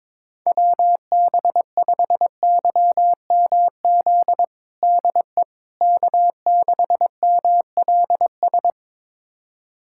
Morse code, heard as a keyed tone, about 22 wpm, 700 Hz. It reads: WB5YMZ DE K6MLS
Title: USA Callsign DE Callsign 22wpm